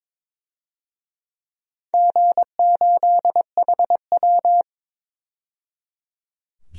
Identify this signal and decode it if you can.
Morse code, heard as a keyed tone, about 22 wpm, 700 Hz. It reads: G8HW
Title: DXCC-100 Call Signs - 4 Characters - 22wpm